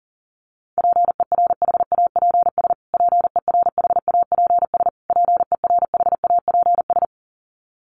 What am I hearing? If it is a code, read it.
Morse code, heard as a keyed tone, about 40 wpm, 700 Hz. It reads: PERHAPS PERHAPS PERHAPS
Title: Top 500 Words Repeated 40wpm